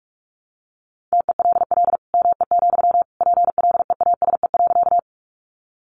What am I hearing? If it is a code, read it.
Morse code, heard as a keyed tone, about 44 wpm, 700 Hz. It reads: TELL ME, PLEASE.
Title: Sentences from Top 1500 Words 45wpm